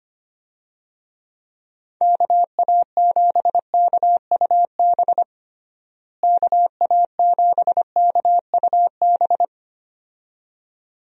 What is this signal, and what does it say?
Morse code, heard as a keyed tone, about 25 wpm, 700 Hz. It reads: KA7KUB KA7KUB
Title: USA Call Signs - 6 Characters - Repeated 2x - 25wpm